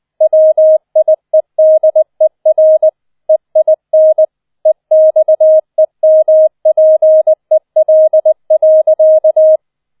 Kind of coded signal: Morse code